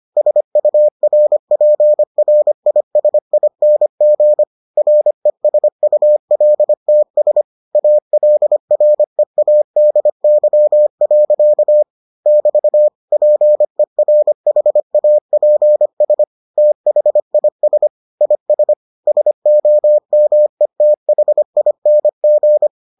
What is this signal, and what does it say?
Morse code, heard as a keyed tone, about 25 wpm, 600 Hz. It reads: SURPRISING RESULTS ALREADY. <BT> PERHAPS THIS IS SOMETHING